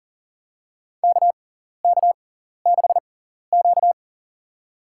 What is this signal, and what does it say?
Morse code, heard as a keyed tone, about 40 wpm, spaced as for 14 wpm, 700 Hz. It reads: KK6Q
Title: USA Call Signs - 4 Characters - 6x Character Spacing - 40wpm